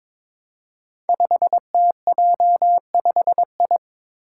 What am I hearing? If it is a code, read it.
Morse code, heard as a keyed tone, about 21 wpm, 700 Hz. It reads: 5TJ5I
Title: Five Challenging Letter-Number Combinations 22wpm